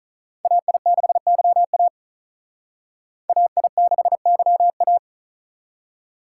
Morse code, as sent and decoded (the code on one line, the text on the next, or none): .- .. -.... -.-- .- / .- .. -.... -.-- .-
AI6YA AI6YA